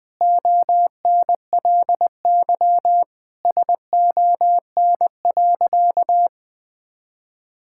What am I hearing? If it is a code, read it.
Morse code, heard as a keyed tone, about 20 wpm, 700 Hz. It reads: ONLY SON.
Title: Sentences with 8-Word Phrases - 20wpm